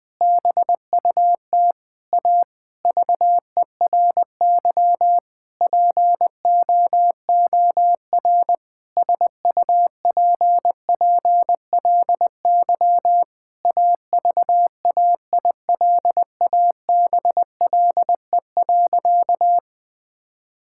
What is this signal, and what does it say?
Morse code, heard as a keyed tone, about 20 wpm, 700 Hz. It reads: BUT A VERY POOR SUPPLY AVAILABLE.
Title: Sentences from Top 1000 Words - Review Part 5 - 20wpm